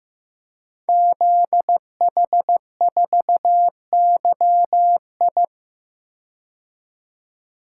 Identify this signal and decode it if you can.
Morse code, heard as a keyed tone, about 15 wpm, 700 Hz. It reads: ZH4YI
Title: Code Groups - 5 Letter:Numbers - 15wpm